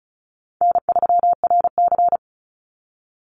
Morse code, heard as a keyed tone, about 34 wpm, 700 Hz. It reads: N3RC
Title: Call Signs Repeated 2x - 35wpm